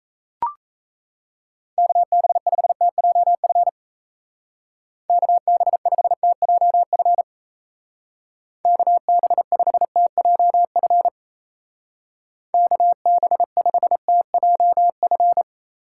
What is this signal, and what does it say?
Morse code, heard as a keyed tone, about 35 wpm, 700 Hz. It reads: KB5TJF KB5TJF KB5TJF KB5TJF